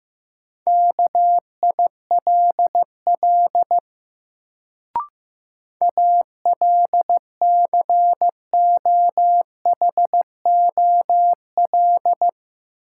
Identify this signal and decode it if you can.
Morse code, heard as a keyed tone, about 15 wpm, 700 Hz. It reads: KILL ALCOHOL
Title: Sentences from Top 1400 Words - New Word Drill Alt - 15wpm